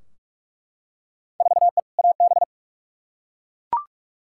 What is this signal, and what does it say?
Morse code, heard as a keyed tone, about 41 wpm, 700 Hz. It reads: 4E AB
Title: Field Day Exchanges Encore - 45wpm